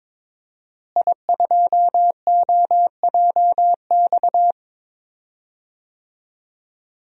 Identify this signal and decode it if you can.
Morse code, heard as a keyed tone, about 22 wpm, 700 Hz. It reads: I2OJX
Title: DXCC-100 Call Signs - 5 Characters - Repeated 3x - 22wpm